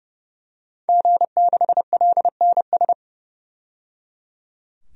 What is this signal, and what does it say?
Morse code, heard as a keyed tone, about 32 wpm, 700 Hz. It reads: G6LNS
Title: DXCC-100 Call Signs - 5 Characters - 30wpm